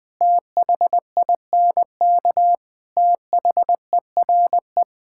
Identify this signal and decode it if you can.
Morse code, heard as a keyed tone, about 20 wpm, 700 Hz. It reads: THINK THERE
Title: Sentences with 4-Word Phrases - 20wpm